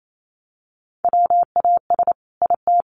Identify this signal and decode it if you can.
Morse code, heard as a keyed tone, about 28 wpm, 700 Hz. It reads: WAS IT
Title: Sentences from Top 1100 Words 28wpm